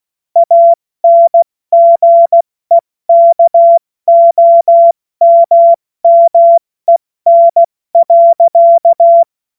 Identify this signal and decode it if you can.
Morse code, heard as a keyed tone, about 16 wpm, spaced as for 14 wpm, 680 Hz. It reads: ANGEKOMMEN.
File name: Morsekurs / Witze / Witz 27 Mai 2025 70 F80